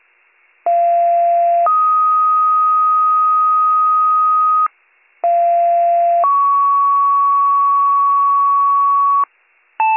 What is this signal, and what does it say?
Two-tone page, heard as a one-second tone, then a three-second tone, about 690 then 1250 Hz, about 690 then 1120 Hz. It is heard from 0.7 to 9.2 s.